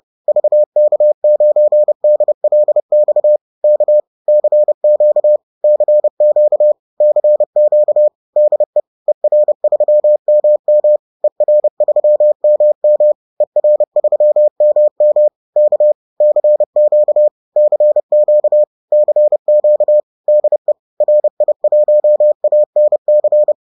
Morse code, heard as a keyed tone, about 30 wpm, 600 Hz. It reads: VK9DLX K CQ CQ CQ DE ER3MM ER3MM ER3MM K CQ CQ CQ DE RI1ANC